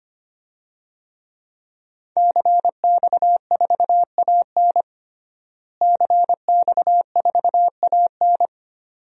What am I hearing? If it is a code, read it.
Morse code, heard as a keyed tone, about 25 wpm, 700 Hz. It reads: CX4AN CX4AN
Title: DXCC-100 Call Signs - 5 Characters - Repeated 2x - 25wpm